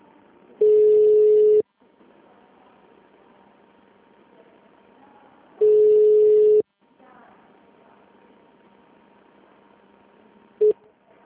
Ringback tone (Europe, 425 Hz).
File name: out- -2005- - - .902